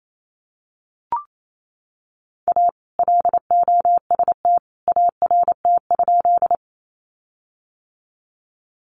Morse code, heard as a keyed tone, about 28 wpm, 700 Hz. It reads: A LOST ART?
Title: Sentences from Top 1400 Words 28wpm